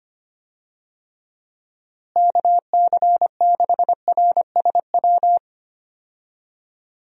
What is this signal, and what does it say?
Morse code, heard as a keyed tone, about 25 wpm, 700 Hz. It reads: KC6RSW
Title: USA Call Signs - 6 Characters - Repeated 3x - 25wpm